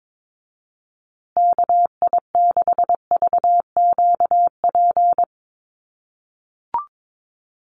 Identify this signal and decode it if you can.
Morse code, heard as a keyed tone, about 22 wpm, 700 Hz. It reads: KI6VQP
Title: USA Call Signs Repeated 3x 22wpm